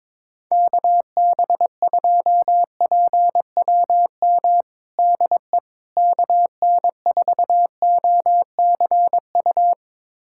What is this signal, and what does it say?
Morse code, heard as a keyed tone, about 22 wpm, 700 Hz. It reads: KB2PWM DE KN4OCU
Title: USA Callsign DE Callsign - 6 Characters - 22wpm